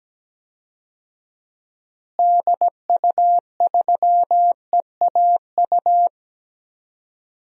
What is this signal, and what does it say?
Morse code, heard as a keyed tone, about 17 wpm, 700 Hz. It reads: DU3EAU
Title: DXCC-100 Call Signs - 6 Characters - Repeated 2x - 17wpm